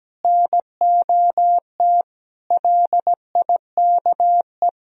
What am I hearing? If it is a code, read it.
Morse code, heard as a keyed tone, about 17 wpm, 700 Hz. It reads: NOT LIKE
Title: Sentences from Top 1000 Words - Review Part 3 - 17wpm